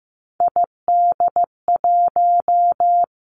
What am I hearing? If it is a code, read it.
Morse code, heard as a keyed tone, about 15 wpm, 700 Hz. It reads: ID1
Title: Three Challenging Letter-Number Combinations 15wpm